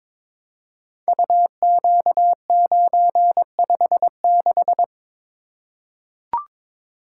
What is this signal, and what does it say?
Morse code, heard as a keyed tone, about 23 wpm, 700 Hz. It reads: UQ956
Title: Code Groups - 5 Letter:Numbers - 22wpm